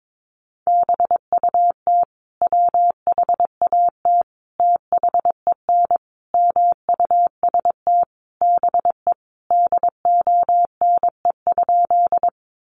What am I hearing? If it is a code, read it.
Morse code, heard as a keyed tone, about 22 wpm, 700 Hz. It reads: BUT WHAT THEN MUST BE DONE?
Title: Sentences from Top 500 Words 22wpm